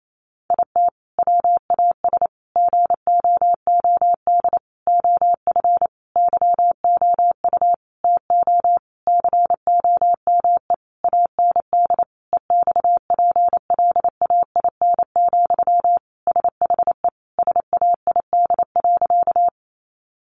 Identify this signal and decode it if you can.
Morse code, heard as a keyed tone, about 28 wpm, 700 Hz. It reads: IT WAS GOOD OF YOU TO COME AND EXPLAIN, SHE SAID.
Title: Sentences from Top 1100 Words 28wpm